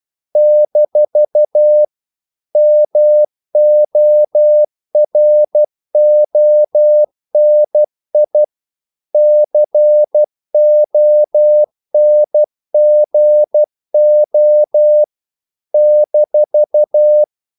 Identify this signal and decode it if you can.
Morse code, heard as a keyed tone, about 12 wpm, 600 Hz. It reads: - MORONI CONGO -
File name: CW / Practice / NationalCapitals12